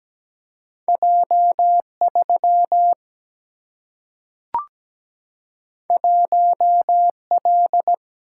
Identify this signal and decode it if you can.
Morse code, heard as a keyed tone, about 17 wpm, 700 Hz. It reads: J3 1L
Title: Single-Letter Single-Number 17wpm